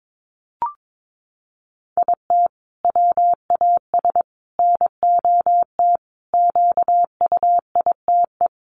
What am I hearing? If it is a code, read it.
Morse code, heard as a keyed tone, about 22 wpm, 700 Hz. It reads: IT WAS NOT QUITE
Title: Sentences from Top 700 Words 22wpm